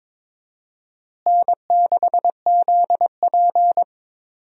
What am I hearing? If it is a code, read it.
Morse code, heard as a keyed tone, about 21 wpm, 700 Hz. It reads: N6ZP
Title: USA Call Signs - 4 Characters - Repeated 3x - 22wpm